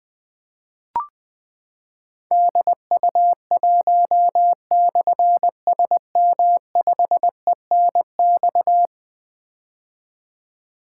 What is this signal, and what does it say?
Morse code, heard as a keyed tone, about 20 wpm, 700 Hz. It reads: DU1/SM5ENX
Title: Difficult Call Signs 20wpm